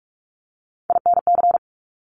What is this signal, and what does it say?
Morse code, heard as a keyed tone, about 46 wpm, 700 Hz. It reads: INC
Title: Top Three Letter Combinations in Words 45wpm